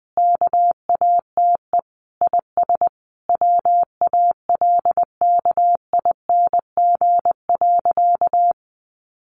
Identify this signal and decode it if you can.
Morse code, heard as a keyed tone, about 20 wpm, 700 Hz. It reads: KATE IS WALKING.
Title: Easy Phrases 20wpm